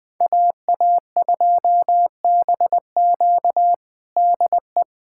Morse code, heard as a keyed tone, about 20 wpm, 700 Hz. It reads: AA2BQ DE
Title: USA Callsign DE Callsign - 5 Characters - 20wpm